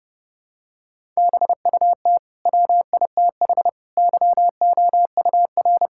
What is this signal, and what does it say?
Morse code, heard as a keyed tone, about 30 wpm, 700 Hz. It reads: BUT WITH YOUR
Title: Sentences from Top 1000 Words - Review Part 6 - 30wpm